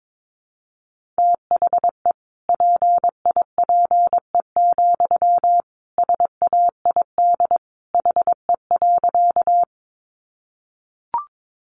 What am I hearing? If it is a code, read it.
Morse code, heard as a keyed tone, about 22 wpm, 700 Hz. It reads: THE PIPE, SAID HE.
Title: Sentences from Top 1500 Words 22wpm